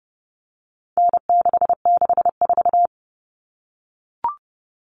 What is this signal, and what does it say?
Morse code, heard as a keyed tone, about 30 wpm, 700 Hz. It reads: N664
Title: All Cut Numbers - 4-Digit 30wpm